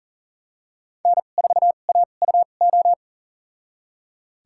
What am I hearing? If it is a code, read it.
Morse code, heard as a keyed tone, about 40 wpm, spaced as for 29 wpm, 700 Hz. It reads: N4AUO
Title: USA Call Signs - 5 Characters - 2x Character Spacing - 40wpm